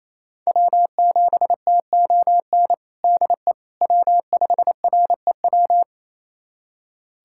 Morse code, heard as a keyed tone, about 28 wpm, 700 Hz. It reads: W7TON DE W5REW
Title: USA Callsign DE Callsign - 5 Characters - 28wpm